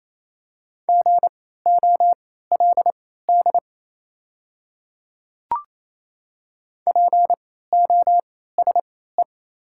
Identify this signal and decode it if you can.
Morse code, heard as a keyed tone, about 27 wpm, spaced as for 16 wpm, 690 Hz. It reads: GOLD POSE
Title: 3x Character Spacing - Top 500 Words - 28wpm